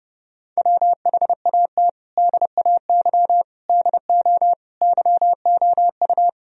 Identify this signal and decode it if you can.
Morse code, heard as a keyed tone, about 30 wpm, 700 Hz. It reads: WHAT DAY DO YOU
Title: Sentences from Top 1000 Words - Review Part 3 - 30wpm